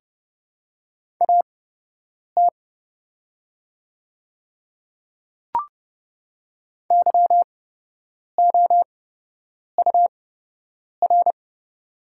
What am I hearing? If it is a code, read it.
Morse code, heard as a keyed tone, about 30 wpm, spaced as for 8 wpm, 700 Hz. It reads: AT YOUR
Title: 8x Character Spacing - Top 100 Words - 30wpm